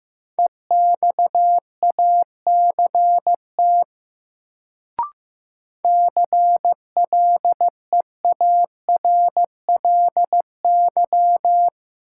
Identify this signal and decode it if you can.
Morse code, heard as a keyed tone, about 15 wpm, 700 Hz. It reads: EXACT CLEARLY
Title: Sentences from Top 1300 Words - New Word Drill - 15wpm